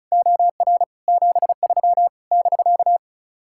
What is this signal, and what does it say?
Morse code, heard as a keyed tone, about 44 wpm, 700 Hz. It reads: OR 73 <BK>
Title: Speed-Racing Common QSO Phrases - 35wpm